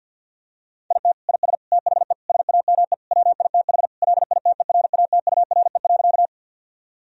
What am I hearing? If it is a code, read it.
Morse code, heard as a keyed tone, about 50 wpm, 700 Hz. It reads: IT IS THE SAME WITH LITERATURE.
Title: Sentences from Top 1000 Words - Review Part 3 - 50wpm